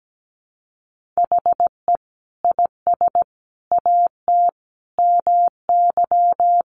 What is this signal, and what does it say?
Morse code, heard as a keyed tone, about 17 wpm, 700 Hz. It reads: HE IS AT MY
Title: Sentences from Top 200 Words 17wpm